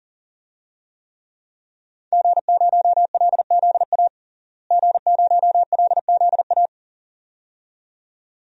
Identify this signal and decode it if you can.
Morse code, heard as a keyed tone, about 40 wpm, 700 Hz. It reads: G0LZA G0LZA
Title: DXCC-100 Call Signs - 5 Characters - Repeated 2x - 40wpm